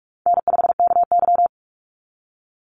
Morse code, heard as a keyed tone, about 46 wpm, 700 Hz. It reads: N5KY
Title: Call Signs - 45wpm